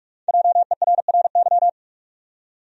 Morse code, heard as a keyed tone, about 47 wpm, 700 Hz. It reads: JERRY
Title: USA Common Names - Max 5 Letters - 45wpm